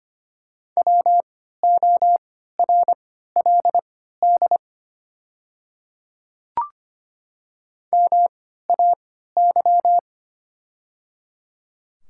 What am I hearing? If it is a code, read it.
Morse code, heard as a keyed tone, about 25 wpm, spaced as for 14 wpm, 700 Hz. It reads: WORLD MAY
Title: 3x Character Spacing - Top 300 Words - 25wpm